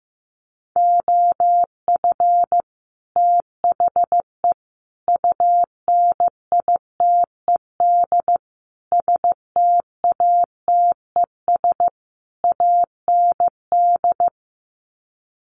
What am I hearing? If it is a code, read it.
Morse code, heard as a keyed tone, about 15 wpm, 700 Hz. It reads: OF THE UNITED STATES AND
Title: Sets of 5 Words 15wpm